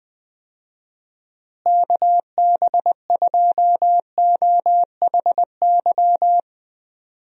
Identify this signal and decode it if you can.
Morse code, heard as a keyed tone, about 21 wpm, 700 Hz. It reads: KB2OHY
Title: USA Call Signs - 6 Characters - Repeated 3x - 20wpm